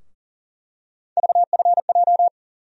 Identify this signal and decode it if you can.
Morse code, heard as a keyed tone, about 40 wpm, 700 Hz. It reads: VFJ
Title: USA Call Sign Postfixes - 3 Letters 40wpm